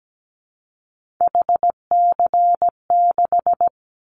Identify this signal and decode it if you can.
Morse code, heard as a keyed tone, about 17 wpm, 700 Hz. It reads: HC6
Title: Callsign Prefixes for Top 100 DXCC - 17wpm